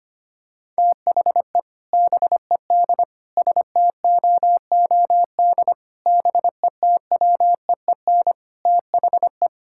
Morse code, heard as a keyed tone, about 25 wpm, 700 Hz. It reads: THE BED STOOD BETWEEN THE
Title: Sentences from Top 1600 Words 25wpm